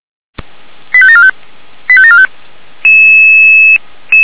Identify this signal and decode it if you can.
Five-tone selective call (ZVEI), heard as quick runs of five tones, tones of about 70 ms. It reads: 85745 85745